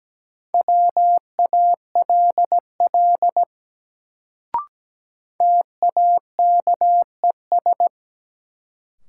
Morse code, heard as a keyed tone, about 17 wpm, 700 Hz. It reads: WALL TAKES
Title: Sentences from Top 1000 Words - Review Part 6 - New Word Drill - 17wpm